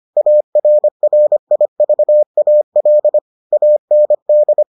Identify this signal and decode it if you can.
Morse code, heard as a keyed tone, about 25 wpm, 600 Hz. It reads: ARRIVAL AND